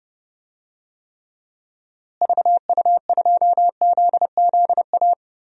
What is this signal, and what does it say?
Morse code, heard as a keyed tone, about 32 wpm, 700 Hz. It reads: VU2ZZA
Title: DXCC-100 Call Signs - 6 Characters - Repeated 3x - 30wpm